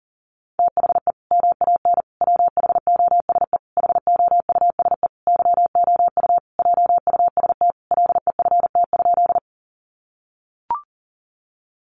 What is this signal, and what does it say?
Morse code, heard as a keyed tone, about 40 wpm, 700 Hz. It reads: THE MAN WHOSE HOUSE YOU JUST LEFT?
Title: Sentences from Top 1200 Words 40wpm